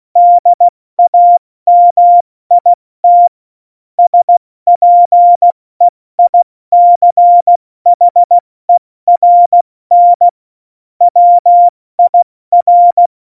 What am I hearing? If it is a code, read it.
Morse code, heard as a keyed tone, about 16 wpm, 700 Hz. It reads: DAMIT SPEICHERN WIR
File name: Morsekurs / Witze / Witz 06. November 2018, 70 F80